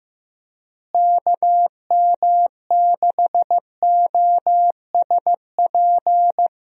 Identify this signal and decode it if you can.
Morse code, heard as a keyed tone, about 15 wpm, 700 Hz. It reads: KM6OSP
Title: USA Call Signs - 6 Characters - Repeated 2x - 15wpm